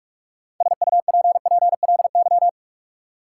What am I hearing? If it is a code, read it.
Morse code, heard as a keyed tone, about 46 wpm, 700 Hz. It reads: SUPPLY
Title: Sentences from Top 1000 Words - Review Part 4 - New Word Drill - 45wpm